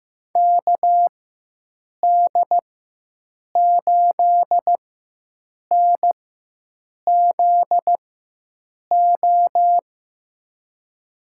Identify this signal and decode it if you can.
Morse code, heard as a keyed tone, about 15 wpm, spaced as for 7 wpm, 700 Hz. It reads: KD8NZO
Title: USA Call Signs - 6 Characters - 4x Character Spacing - 15wpm